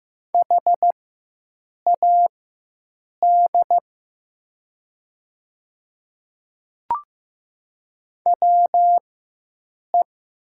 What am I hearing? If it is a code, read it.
Morse code, heard as a keyed tone, about 15 wpm, spaced as for 7 wpm, 700 Hz. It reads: HAD WE
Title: 4x Character Spacing - Top 100 Words - 15wpm